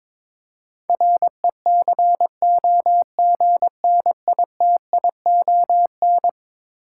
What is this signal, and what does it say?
Morse code, heard as a keyed tone, about 21 wpm, 700 Hz. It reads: RECOGNITION
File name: practice / 22wpm / Eleven Letter Words - 22wpm